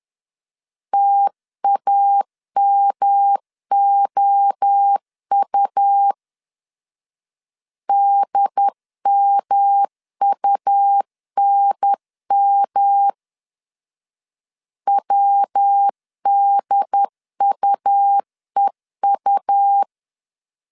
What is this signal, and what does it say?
Morse code, heard as a keyed tone, about 10 wpm, 800 Hz. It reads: TAMOU DMUNM WDUEU
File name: CW / DET / 1 3 1500C S A 10wpm 3447